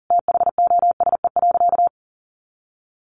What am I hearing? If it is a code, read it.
Morse code, heard as a keyed tone, about 40 wpm, 700 Hz. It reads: THOSE.
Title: A Sound Of Thunder 40wpm